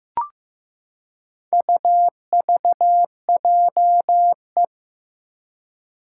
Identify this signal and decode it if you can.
Morse code, heard as a keyed tone, about 15 wpm, 700 Hz. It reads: UVJE
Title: Four Challenging Letter-Number Combinations 15wpm